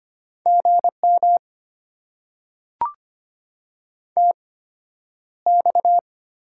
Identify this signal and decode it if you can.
Morse code, heard as a keyed tone, about 25 wpm, spaced as for 11 wpm, 700 Hz. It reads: GM TX
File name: Top 90 QSO Elements - 8x Character Spacing - 25wpm